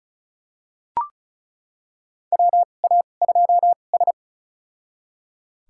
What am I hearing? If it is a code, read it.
Morse code, heard as a keyed tone, about 36 wpm, spaced as for 25 wpm, 700 Hz. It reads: WA2S
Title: USA Call Signs - 4 Characters - 2x Character Spacing - 35wpm